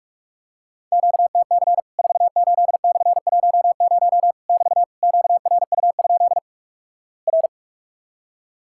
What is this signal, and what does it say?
Morse code, heard as a keyed tone, about 46 wpm, 700 Hz. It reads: QTC 48/10 <BT> QRU? R
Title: Worked All Europe DX Contest - QTC Exchange - 45wpm